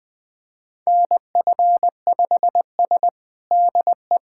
Morse code, heard as a keyed tone, about 20 wpm, 700 Hz. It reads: NF5S DE
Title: USA Callsign DE Callsign - 4 Characters - 20wpm